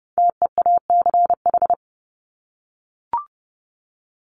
Morse code, heard as a keyed tone, about 30 wpm, 700 Hz. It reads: TEACH
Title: ING Suffix - Top 100 Words with Base Words 30wpm